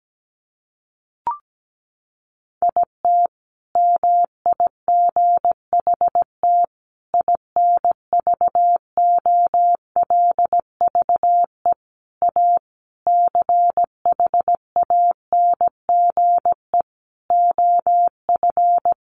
Morse code, heard as a keyed tone, about 17 wpm, 700 Hz. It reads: IT MIGHT INVOLVE A CHANGE OF
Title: Sentences from Top 1200 Words 17wpm